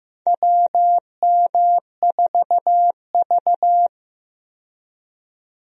Morse code, heard as a keyed tone, about 15 wpm, 700 Hz. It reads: WM4V
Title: USA Call Signs - 4 Characters - 15wpm